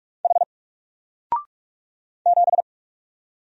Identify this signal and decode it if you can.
Morse code, heard as a keyed tone, about 44 wpm, 700 Hz. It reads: H 7
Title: Single Letter-Number 45wpm